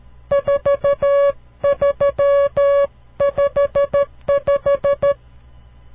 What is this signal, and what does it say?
Morse code, heard as a keyed tone, about 13 wpm, 560 Hz. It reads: 4355
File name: CW Learning Programs / farnsworth / unit 5